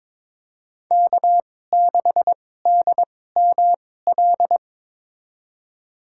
Morse code, heard as a keyed tone, about 22 wpm, spaced as for 16 wpm, 700 Hz. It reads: K6DML
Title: USA Call Signs - 5 Characters - 2x Character Spacing - 22wpm